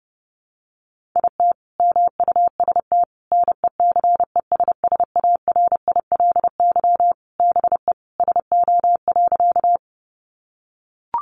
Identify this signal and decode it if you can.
Morse code, heard as a keyed tone, about 30 wpm, 700 Hz. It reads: IT MUST NECESSARILY BE SO.
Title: Sentences from Top 1400 Words 30wpm